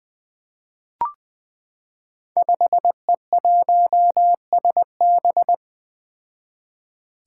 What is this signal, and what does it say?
Morse code, heard as a keyed tone, about 20 wpm, 700 Hz. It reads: 5E1SB
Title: Five Challenging Letter-Number Combinations 20wpm